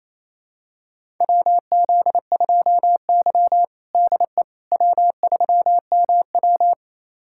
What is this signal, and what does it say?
Morse code, heard as a keyed tone, about 28 wpm, 700 Hz. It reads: WZ2Y DE W3MW
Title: USA Callsign DE Callsign - 4 Characters - 28wpm